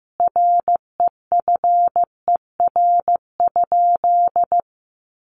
Morse code, heard as a keyed tone, about 15 wpm, 700 Hz. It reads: REFER?
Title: Sentences from Top 1000 Words 15wpm